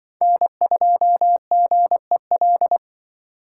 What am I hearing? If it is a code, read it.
Morse code, heard as a keyed tone, about 26 wpm, 700 Hz. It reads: N2GEL